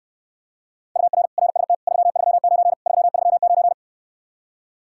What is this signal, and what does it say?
Morse code, heard as a keyed tone, about 69 wpm, 700 Hz. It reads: UR RST 339 339